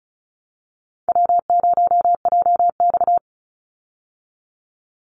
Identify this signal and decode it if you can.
Morse code, heard as a keyed tone, about 34 wpm, 700 Hz. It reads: W0JX
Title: Easy Call Signs Repeated 2x - 35wpm